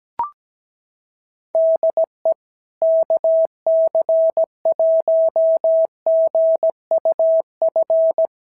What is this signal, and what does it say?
Morse code, heard as a keyed tone, about 17 wpm, 650 Hz. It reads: DE KC1GUF
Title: SOTA QSO Practice 17wpm